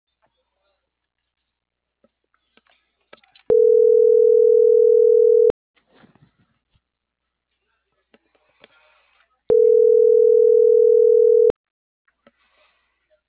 Ringback tone (North America), 440 + 480 Hz.
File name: CENTRAL MAERSK / out yunuel.morales - .34286